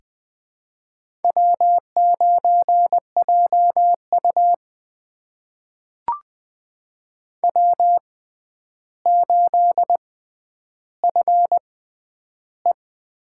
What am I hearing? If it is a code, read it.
Morse code, heard as a keyed tone, about 20 wpm, spaced as for 10 wpm, 700 Hz. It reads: W9JU W8FE
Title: USA Call Signs - 6x Character Spacing - 20wpm